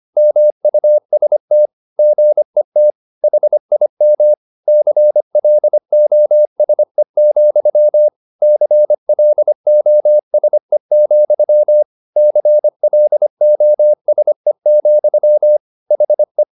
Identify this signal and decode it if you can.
Morse code, heard as a keyed tone, about 25 wpm, 600 Hz. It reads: MUST GET HIM CLOSE, CLOSE, CLOSE, HE